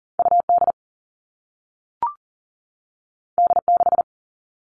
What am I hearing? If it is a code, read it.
Morse code, heard as a keyed tone, about 38 wpm, 700 Hz. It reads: UD D6
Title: All Cut Numbers - 2-Digit 40wpm